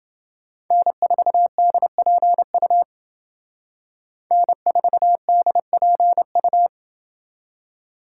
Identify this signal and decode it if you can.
Morse code, heard as a keyed tone, about 28 wpm, 700 Hz. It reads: N4DPU N4DPU